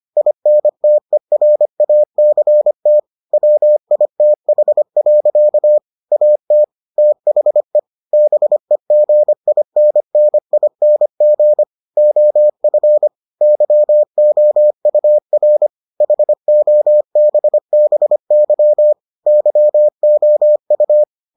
Morse code, heard as a keyed tone, about 25 wpm, 600 Hz. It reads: INTERACT WITH. AT THE BEGINNING OF YOUR HOBBY YOU